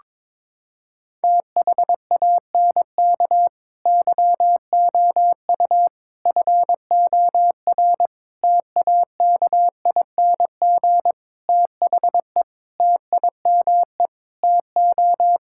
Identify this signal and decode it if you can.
Morse code, heard as a keyed tone, about 22 wpm, 700 Hz. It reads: THANK YOU FOR TAKING THE TIME TO
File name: Sets of 9 Words - 22wpm